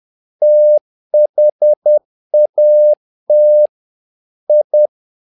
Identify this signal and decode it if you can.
Morse code, heard as a keyed tone, about 10 wpm, 600 Hz. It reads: THAT I
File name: learning center / tower trouble / tower trouble 10wpm